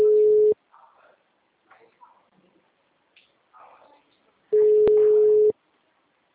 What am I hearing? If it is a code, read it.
Ringback tone (Europe, 425 Hz), with speech.
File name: out- -2012- - - .67462